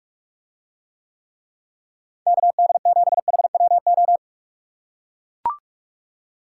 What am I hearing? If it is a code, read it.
Morse code, heard as a keyed tone, about 44 wpm, 700 Hz. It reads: KD7HWO